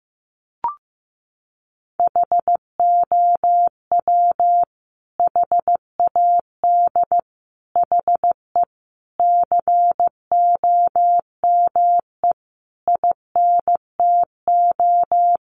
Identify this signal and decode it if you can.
Morse code, heard as a keyed tone, about 15 wpm, 700 Hz. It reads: HOW HAD HE COME INTO
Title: Sentences from Top 1300 Words 15wpm